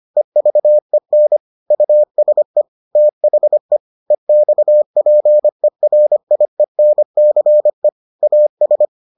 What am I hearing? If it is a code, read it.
Morse code, heard as a keyed tone, about 25 wpm, 600 Hz. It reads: EVEN USE THE EXPERIENCE AS